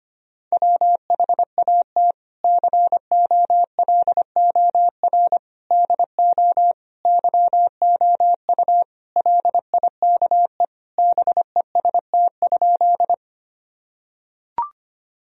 Morse code, heard as a keyed tone, about 25 wpm, 700 Hz. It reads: WHAT COLOR DO YOU LIKE BEST?
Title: Sentences from Top 1000 Words - Review Part 2 - 25wpm